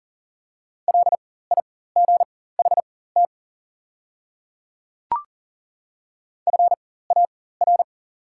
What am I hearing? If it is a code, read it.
Morse code, heard as a keyed tone, about 40 wpm, spaced as for 19 wpm, 690 Hz. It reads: LIGHT FAR
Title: 4x Character Spacing - Top 500 Words - 40wpm